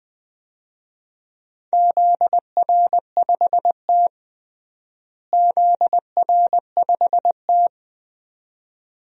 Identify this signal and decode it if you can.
Morse code, heard as a keyed tone, about 20 wpm, 700 Hz. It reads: ZR5T ZR5T
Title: DXCC-100 Call Signs - 4 Characters - Repeated 2x - 20wpm